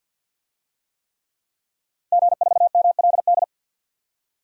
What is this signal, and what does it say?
Morse code, heard as a keyed tone, about 47 wpm, 700 Hz. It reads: G4MLD
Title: DXCC-100 Call Signs - 5 Characters - Repeated 3x - 50wpm